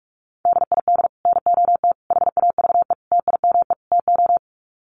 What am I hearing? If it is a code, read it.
Morse code, heard as a keyed tone, about 44 wpm, 700 Hz. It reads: DID NOT HAVE TIME TO
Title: Sets of 5 Words Encore 45wpm